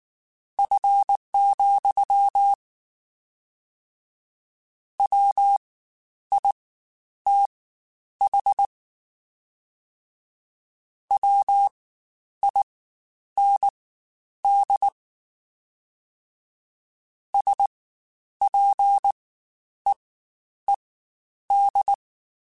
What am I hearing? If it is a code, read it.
Morse code, heard as a keyed tone, about 19 wpm, spaced as for 9 wpm, 800 Hz. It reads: F, WITH WIND SPEED
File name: Spaced Characters / Spaced Characters-mp3s / Weather 5 1 sp @18